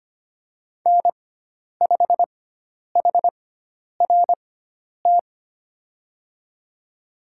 Morse code, heard as a keyed tone, about 25 wpm, spaced as for 10 wpm, 700 Hz. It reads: N5HRT
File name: Lesson 16g - Letter c - Call Signs 25wpm